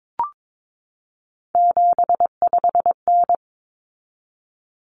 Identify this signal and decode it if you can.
Morse code, heard as a keyed tone, about 22 wpm, 700 Hz. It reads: 75N
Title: Common Cut Numbers - 3-Digit 22wpm